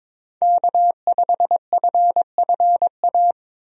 Morse code, heard as a keyed tone, about 22 wpm, 700 Hz. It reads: K5FFA
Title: USA Call Signs - 5 Characters - Repeated 3x - 22wpm